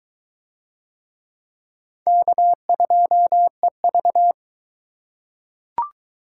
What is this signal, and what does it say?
Morse code, heard as a keyed tone, about 21 wpm, 700 Hz. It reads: K2EV